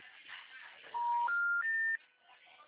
Special information tone.